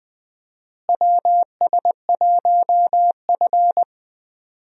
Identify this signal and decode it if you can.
Morse code, heard as a keyed tone, about 20 wpm, 700 Hz. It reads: WS1F
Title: USA Call Signs - 4 Characters - Repeated 2x - 20wpm